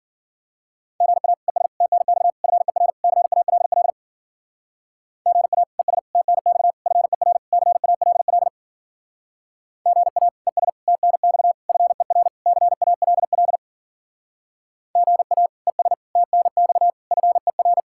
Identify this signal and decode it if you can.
Morse code, heard as a keyed tone, about 50 wpm, 700 Hz. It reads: GA ES TNX FER CALL GA ES TNX FER CALL GA ES TNX FER CALL GA ES TNX FER
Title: Speed-Racing Common QSO Phrases - 40wpm